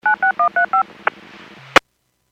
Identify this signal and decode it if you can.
DTMF, heard as telephone touch tones, tones of about 110 ms. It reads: 96135